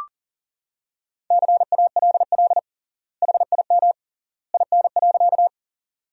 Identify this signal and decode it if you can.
Morse code, heard as a keyed tone, about 40 wpm, 700 Hz. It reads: CALL HIM IN.
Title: 3x Word-Spacing - Sentences from Top 100 Words - 40wpm